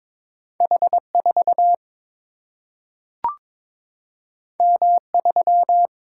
Morse code, heard as a keyed tone, about 22 wpm, 700 Hz. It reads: H 4 M 3
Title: Single-Letter Single-Number 22wpm